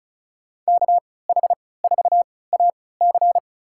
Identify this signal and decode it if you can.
Morse code, heard as a keyed tone, about 36 wpm, spaced as for 20 wpm, 700 Hz. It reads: KH4AC
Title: USA Call Signs - 5 Characters - 3x Character Spacing - 35wpm